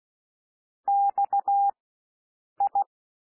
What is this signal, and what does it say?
Morse code, heard as a keyed tone, about 16 wpm, spaced as for 7 wpm, 800 Hz. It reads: XI
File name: cwmp3 / W1 SOUND WORDS 16 wpm